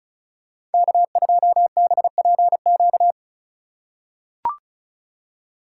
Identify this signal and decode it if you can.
Morse code, heard as a keyed tone, about 34 wpm, 700 Hz. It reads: K2BPQ
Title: USA Call Signs - 5 Characters - 3x Character Spacing - 35wpm